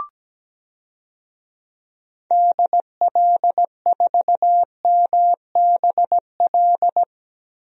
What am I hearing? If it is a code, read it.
Morse code, heard as a keyed tone, about 17 wpm, 700 Hz. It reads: DL4MBL
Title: DXCC-100 Call Signs - 6 Characters - 17wpm